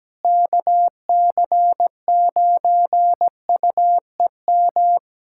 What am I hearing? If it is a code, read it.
Morse code, heard as a keyed tone, about 17 wpm, 700 Hz. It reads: KC9UEM
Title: USA Callsign DE Callsign - 6 Characters - 17wpm